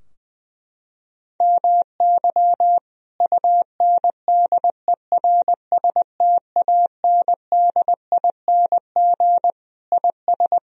Morse code, heard as a keyed tone, about 20 wpm, 700 Hz. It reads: MY UNDERSTANDING IS
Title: Sentences from Top 1000 Words - Review Part 3 - 20wpm